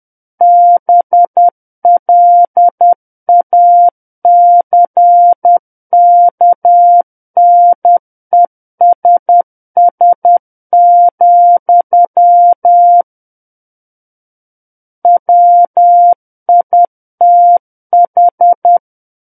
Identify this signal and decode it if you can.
Morse code, heard as a keyed tone, about 10 wpm, 700 Hz. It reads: BLACKNESS, WITH
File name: Book One / 01-Chapter 1 - 10 WPM